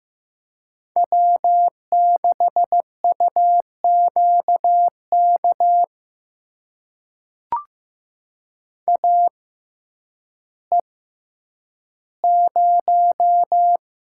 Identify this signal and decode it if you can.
Morse code, heard as a keyed tone, about 15 wpm, spaced as for 9 wpm, 700 Hz. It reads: W6UQK AE0
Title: USA Call Signs - 5 Characters - 6x Character Spacing - 15wpm